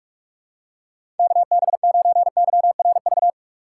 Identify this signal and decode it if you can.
Morse code, heard as a keyed tone, about 46 wpm, 700 Hz. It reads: KB9YRV
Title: USA Call Signs - 6 Characters - 3x Character Spacing - 45wpm